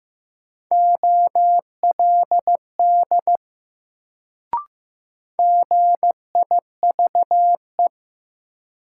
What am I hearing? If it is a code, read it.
Morse code, heard as a keyed tone, about 15 wpm, 700 Hz. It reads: OLD GIVE
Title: Sets of 2 Words - New Word Drill - 15wpm